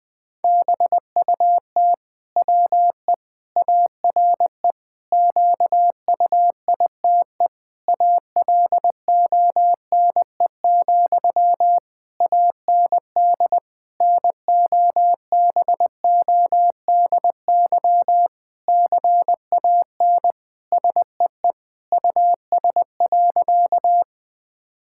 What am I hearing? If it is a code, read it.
Morse code, heard as a keyed tone, about 20 wpm, 700 Hz. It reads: BUT WE ARE QUITE ALONE, AND NOBODY CAN SEE US.
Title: Sentences from Top 1600 Words 20wpm